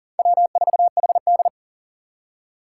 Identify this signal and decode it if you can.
Morse code, heard as a keyed tone, about 51 wpm, 700 Hz. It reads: W4HD